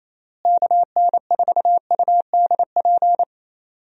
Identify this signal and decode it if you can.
Morse code, heard as a keyed tone, about 28 wpm, 700 Hz. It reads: KN4UDP
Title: USA Call Signs - 6 Characters - Repeated 3x - 28wpm